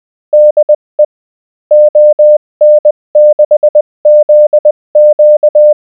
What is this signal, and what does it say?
Morse code, heard as a keyed tone, about 20 wpm, spaced as for 18 wpm, 600 Hz. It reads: DE ON6ZQ